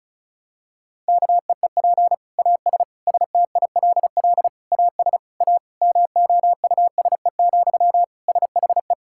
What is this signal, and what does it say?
Morse code, heard as a keyed tone, about 34 wpm, 700 Hz. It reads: KEEP AS STILL AS A MOUSE, SHE
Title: Sentences from Top 1000 Words - Review Part 5 - 35wpm